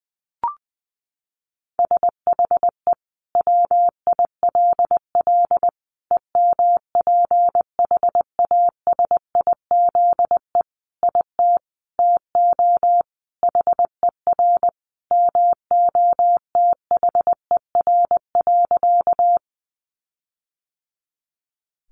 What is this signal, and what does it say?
Morse code, heard as a keyed tone, about 20 wpm, 700 Hz. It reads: SHE WILL EMPHASIZE IT TO HER MOTHER.
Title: Sentences from Top 1500 Words 20wpm